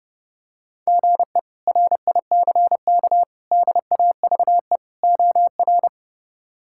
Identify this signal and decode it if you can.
Morse code, heard as a keyed tone, about 30 wpm, 700 Hz. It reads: GE RICK DAVE OR
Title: K1USN SST QSO Practice - 30wpm